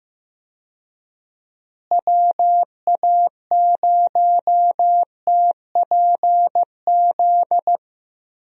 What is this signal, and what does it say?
Morse code, heard as a keyed tone, about 15 wpm, 700 Hz. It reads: WA0TPZ
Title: USA Call Signs - 6 Characters - Repeated 2x - 15wpm